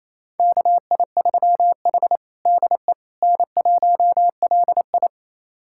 Morse code, heard as a keyed tone, about 27 wpm, 700 Hz. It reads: KI3H DE N1LI
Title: USA Callsign DE Callsign - 4 Characters - 28wpm